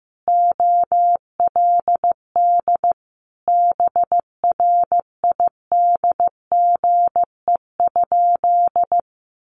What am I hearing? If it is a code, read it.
Morse code, heard as a keyed tone, about 15 wpm, 700 Hz. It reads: OLD BRIDGE?
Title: Sentences from Top 1200 Words 15wpm